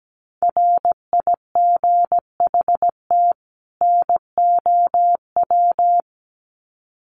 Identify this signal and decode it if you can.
Morse code, heard as a keyed tone, about 17 wpm, 700 Hz. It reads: RIGHT NOW
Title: Sets of 2 Words 17wpm